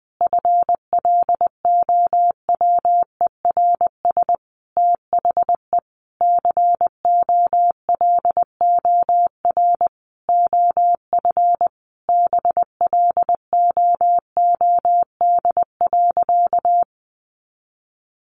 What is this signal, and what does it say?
Morse code, heard as a keyed tone, about 20 wpm, 700 Hz. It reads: FLOWERS THE COLOR OF BLOOD.
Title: A Sound Of Thunder 20wpm